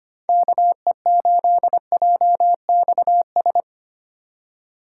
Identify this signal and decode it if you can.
Morse code, heard as a keyed tone, about 25 wpm, 700 Hz. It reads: KE8JXS
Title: USA Call Signs - 6x Character Spacing - 25wpm